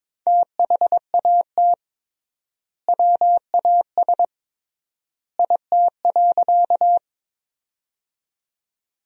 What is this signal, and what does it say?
Morse code, heard as a keyed tone, about 22 wpm, 700 Hz. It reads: THAT WAS IT.
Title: 3x Word-Spacing - Sentences from Top 100 Words - 22wpm